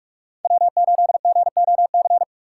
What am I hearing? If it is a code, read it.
Morse code, heard as a keyed tone, about 47 wpm, 700 Hz. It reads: W8GOC
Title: USA Call Signs - 5 Characters - 4x Character Spacing - 45wpm